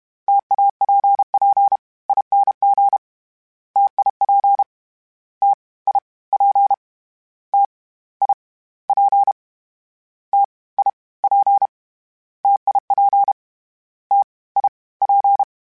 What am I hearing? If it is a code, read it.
Morse code, heard as a keyed tone, about 32 wpm, 800 Hz. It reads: TAPPING TIP T I P T I P T I P TIP T I P
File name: Variable Speed / Variable Speed-mp3s / T AOEI's sp @30